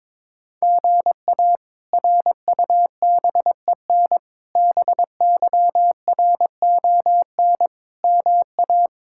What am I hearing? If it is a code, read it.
Morse code, heard as a keyed tone, about 21 wpm, 700 Hz. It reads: GA RUBEN BYRON MA
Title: K1USN SST QSO Practice - 22wpm